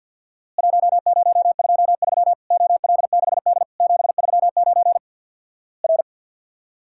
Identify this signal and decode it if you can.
Morse code, heard as a keyed tone, about 50 wpm, 700 Hz. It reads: 1023 OL6D 739 R
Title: Worked All Europe DX Contest - QTC Exchange with Repeats - 50wpm